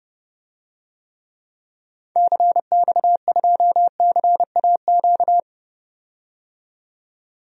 Morse code, heard as a keyed tone, about 30 wpm, 700 Hz. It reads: CX2CAQ
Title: DXCC-100 Call Signs - 6 Characters - 30wpm